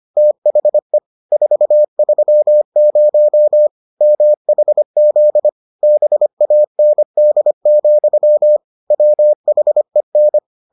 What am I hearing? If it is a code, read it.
Morse code, heard as a keyed tone, about 25 wpm, 600 Hz. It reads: THE 430 MHZ BAND, WHEN